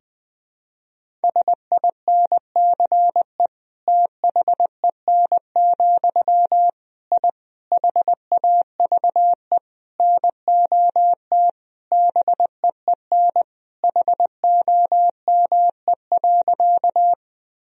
Morse code, heard as a keyed tone, about 21 wpm, 700 Hz. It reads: SINCE THEN, I HAVE NOT BEEN HOME.
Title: Sentences from Top 1000 Words - Review Part 1 - 20wpm